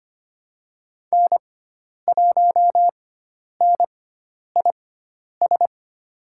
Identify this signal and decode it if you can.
Morse code, heard as a keyed tone, about 25 wpm, spaced as for 10 wpm, 700 Hz. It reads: N1NIS
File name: Lesson 16g - Letter c - Call Signs 25wpm